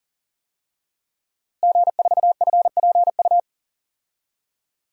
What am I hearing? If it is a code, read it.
Morse code, heard as a keyed tone, about 39 wpm, 700 Hz. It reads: G4FPU
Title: DXCC-100 Call Signs - 5 Characters - Repeated 2x - 40wpm